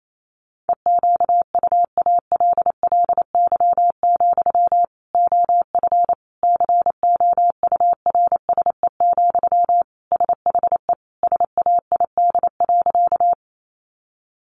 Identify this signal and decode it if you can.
Morse code, heard as a keyed tone, about 28 wpm, 700 Hz. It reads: EQUALLY, OF COURSE, SHE SAID.
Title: Sentences from Top 1400 Words 28wpm